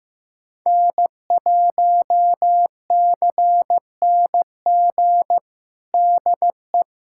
Morse code, heard as a keyed tone, about 15 wpm, 700 Hz. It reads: N1CNG DE
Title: USA Callsign DE Callsign - 5 Characters - 15wpm